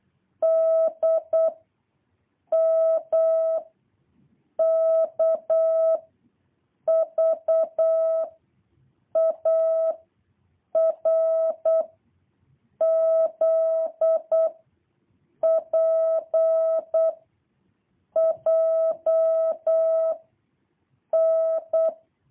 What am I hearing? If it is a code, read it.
Morse code, heard as a keyed tone, about 8 wpm, 650 Hz. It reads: D M K V A R Z P J N
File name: jh8cbh / 2023cw